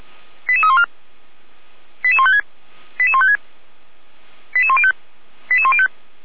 Five-tone selective call (ZVEI), heard as quick runs of five tones, tones of about 70 ms. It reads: 88314 88156 88156 88105 88105